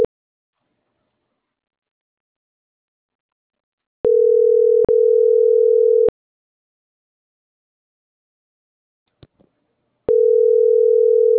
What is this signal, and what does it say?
Ringback tone (North America), 440 + 480 Hz.